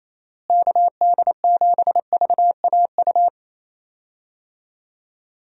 Morse code, heard as a keyed tone, about 28 wpm, 700 Hz. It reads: KD7VAU
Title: USA Call Signs - 6 Characters - Repeated 2x - 28wpm